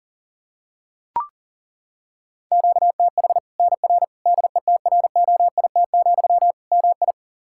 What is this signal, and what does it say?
Morse code, heard as a keyed tone, about 40 wpm, 700 Hz. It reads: QTH NR DETROIT, MI
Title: Practice QSO - Exchange 1 - Beginning - 40wpm